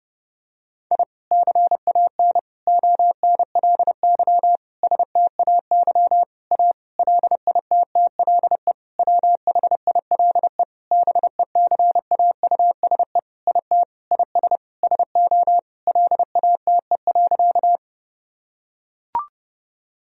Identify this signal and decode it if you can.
Morse code, heard as a keyed tone, about 32 wpm, 700 Hz. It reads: I CAN ONLY STAY A LITTLE WHILE BECAUSE IT IS SO LATE.
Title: Sentences from Top 1000 Words - Review Part 3 - 30wpm